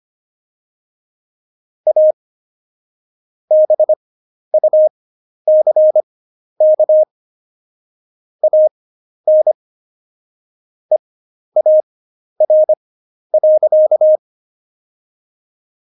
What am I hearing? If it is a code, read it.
Morse code, heard as a keyed tone, about 25 wpm, spaced as for 11 wpm, 620 Hz. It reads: A BUCK AN EAR.
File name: Jokes-25-char-10-eff